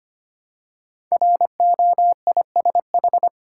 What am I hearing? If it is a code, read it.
Morse code, heard as a keyed tone, about 25 wpm, 700 Hz. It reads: ROISH
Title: Code Groups - 5 Letters - 25wpm